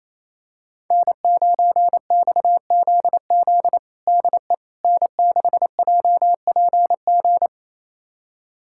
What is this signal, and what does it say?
Morse code, heard as a keyed tone, about 28 wpm, 700 Hz. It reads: N9XZZ DE N6JPG
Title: USA Callsign DE Callsign - 5 Characters - 28wpm